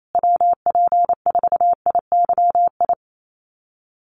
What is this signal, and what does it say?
Morse code, heard as a keyed tone, about 28 wpm, 700 Hz. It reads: WP4IYI
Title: USA Call Signs Repeated 3x 28wpm